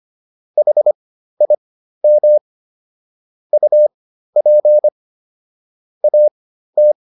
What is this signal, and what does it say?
Morse code, heard as a keyed tone, about 25 wpm, spaced as for 13 wpm, 600 Hz. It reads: HIM UP AT